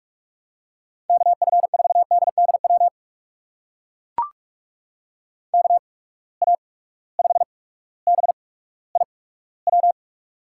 Morse code, heard as a keyed tone, about 41 wpm, spaced as for 19 wpm, 700 Hz. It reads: KF4DDW KA5BIW
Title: USA Call Signs - 6 Characters - 8x Character Spacing - 45wpm